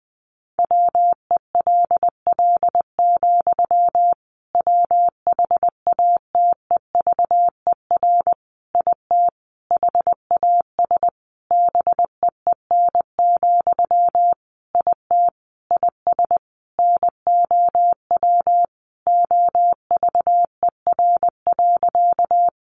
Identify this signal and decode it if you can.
Morse code, heard as a keyed tone, about 20 wpm, 700 Hz. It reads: WELL, WHATEVER IT HAS BEEN, IT IS NOW OVER.
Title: Sentences from Top 1100 Words 20wpm